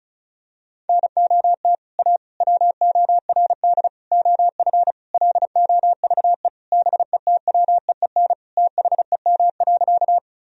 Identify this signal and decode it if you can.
Morse code, heard as a keyed tone, about 34 wpm, 700 Hz. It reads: NOT A WORD OF LOVE BETWEEN THEM.
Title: Sentences from Top 1000 Words - Review Part 1 - 35wpm